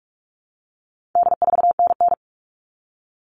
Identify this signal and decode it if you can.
Morse code, heard as a keyed tone, about 44 wpm, 700 Hz. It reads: D4NN
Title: All Cut Numbers - 4-Digit 45wpm